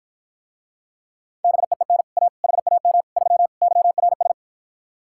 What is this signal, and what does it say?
Morse code, heard as a keyed tone, about 52 wpm, 700 Hz. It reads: BEEN A HAM 3 YRS